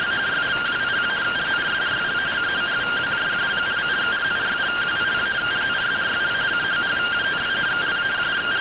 Two-tone teleprinter signal, about 170 Hz shift.
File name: rtty rx